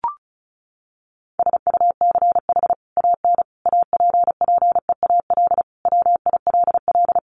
Morse code, heard as a keyed tone, about 34 wpm, 700 Hz. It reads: SUCH AN APPEAL WILL
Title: Sentences from Top 1300 Words 35wpm